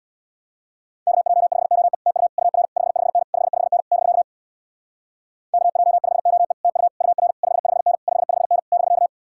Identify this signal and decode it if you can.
Morse code, heard as a keyed tone, about 62 wpm, 700 Hz. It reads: KJ5OE TU UR 55N 55N <BK> KJ5OE TU UR 55N 55N <BK>
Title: Speed-Racing Common QSO Phrases - 50wpm